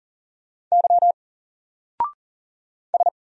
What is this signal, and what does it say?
Morse code, heard as a keyed tone, about 40 wpm, 700 Hz. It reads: Y S
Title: Single Letter-Number Alt 40wpm